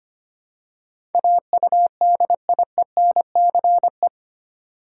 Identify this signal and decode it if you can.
Morse code, heard as a keyed tone, about 25 wpm, 700 Hz. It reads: AUDIENCE
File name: Eight Letter Words - 25wpm